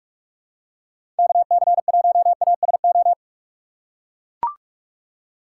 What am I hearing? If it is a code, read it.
Morse code, heard as a keyed tone, about 42 wpm, 700 Hz. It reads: KC1ASO
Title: USA Call Signs - 6x Character Spacing - 45wpm